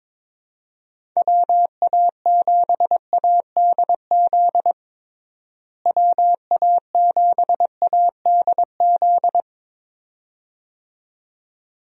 Morse code, heard as a keyed tone, about 22 wpm, 700 Hz. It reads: WA7ADZ WA7ADZ
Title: USA Call Signs - 6 Characters - Repeated 2x - 22wpm